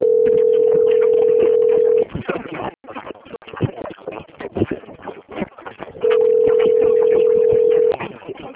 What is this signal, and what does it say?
Ringback tone (North America), 440 + 480 Hz.
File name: out- - - - - .22704